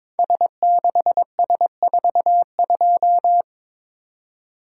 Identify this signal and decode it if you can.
Morse code, heard as a keyed tone, about 22 wpm, 700 Hz. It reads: S6S42
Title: Code Groups - 5 Letter:Numbers - 22wpm